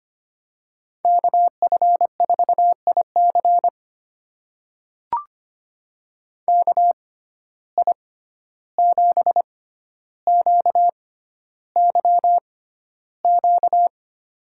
Morse code, heard as a keyed tone, about 25 wpm, spaced as for 12 wpm, 700 Hz. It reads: KF4IC KI7QYQ
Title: USA Call Signs - 6x Character Spacing - 25wpm